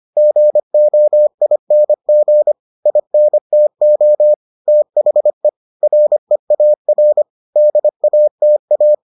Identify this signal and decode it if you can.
Morse code, heard as a keyed tone, about 25 wpm, 600 Hz. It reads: GOING INTO THE REAR DATA